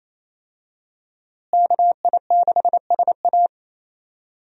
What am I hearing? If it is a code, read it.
Morse code, heard as a keyed tone, about 28 wpm, 700 Hz. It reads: KI6SA
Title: USA Call Signs - 5 Characters - Repeated 2x - 28wpm